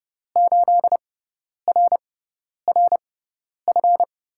Morse code, heard as a keyed tone, about 30 wpm, spaced as for 10 wpm, 700 Hz. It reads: 8RRF
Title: USA Call Signs - 5 Characters - 6x Character Spacing - 30wpm